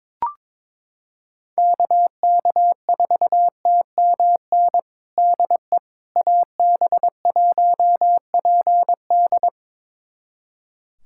Morse code, heard as a keyed tone, about 22 wpm, 700 Hz. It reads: KK4TMN DE AB1PD
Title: USA Callsign DE Callsign 22wpm